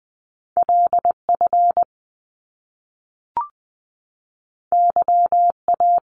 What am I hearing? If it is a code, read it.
Morse code, heard as a keyed tone, about 20 wpm, 700 Hz. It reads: LF YA
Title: Top Two Letter Combinations in Words 20wpm